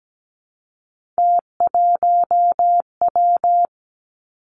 Morse code, heard as a keyed tone, about 17 wpm, 700 Hz. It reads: T1W
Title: Three Challenging Letter-Number Combinations 17wpm